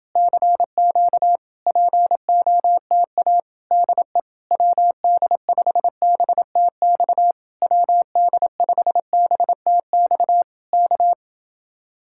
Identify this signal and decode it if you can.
Morse code, heard as a keyed tone, about 29 wpm, 700 Hz. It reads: CQ POTA DE WD5BTX WD5BTX K